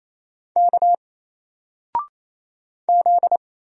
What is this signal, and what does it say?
Morse code, heard as a keyed tone, about 27 wpm, 700 Hz. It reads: K Z
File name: Single-Letter-Number-28wpm